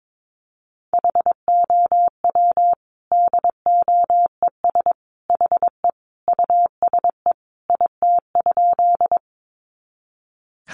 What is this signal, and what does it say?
Morse code, heard as a keyed tone, about 22 wpm, 700 Hz. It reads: HOW DOES HE USE IT?
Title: Sentences from Top 200 Words 22wpm